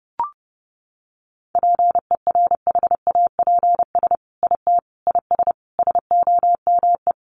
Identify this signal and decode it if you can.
Morse code, heard as a keyed tone, about 30 wpm, 700 Hz. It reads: PERHAPS IT IS SOME
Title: Sentences from Top 800 Words 30wpm